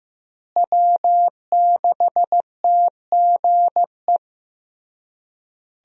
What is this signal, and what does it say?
Morse code, heard as a keyed tone, about 15 wpm, 700 Hz. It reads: W6TGE
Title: USA Call Signs - 5 Characters - Repeated 3x - 15wpm